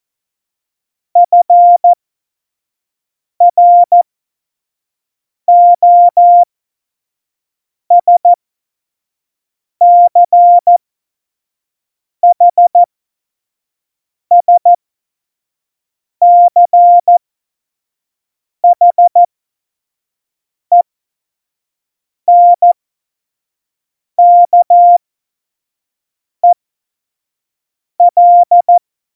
Morse code, heard as a keyed tone, about 14 wpm, spaced as for 5 wpm, 700 Hz. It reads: FROSCHSCHENKEL
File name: Morsekurs / Witze / Witz 12 Mai 2020, 25 F70